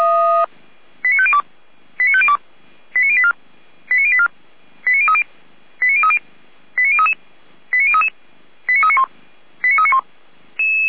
Five-tone selective call (ZVEI), heard as quick runs of five tones, tones of about 70 ms. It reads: 89502 89502 89084 89084 89030 89030 89033 89033 89391 89391
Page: Server at audio.f85360.feuerwehr-orlamuende.org